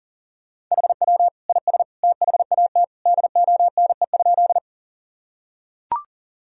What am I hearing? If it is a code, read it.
Morse code, heard as a keyed tone, about 37 wpm, 700 Hz. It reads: HW IS THAT DONE?
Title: Sentences from Top 500 Words and Abbreviations 40wpm